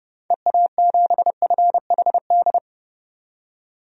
Morse code, heard as a keyed tone, about 30 wpm, 700 Hz. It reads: EA7FHD
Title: DXCC-100 Call Signs - 6 Characters - 30wpm